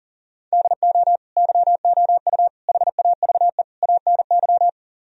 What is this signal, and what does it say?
Morse code, heard as a keyed tone, about 52 wpm, 700 Hz. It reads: DO YOU HAVE ANY